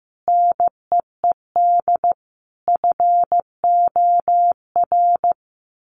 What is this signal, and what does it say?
Morse code, heard as a keyed tone, about 14 wpm, 700 Hz. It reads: NEED FOR
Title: Sets of 5 Words Encore2 15wpm